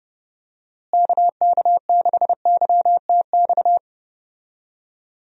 Morse code, heard as a keyed tone, about 32 wpm, 700 Hz. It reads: KK6YTX
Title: USA Call Signs - 6 Characters - Repeated 2x - 30wpm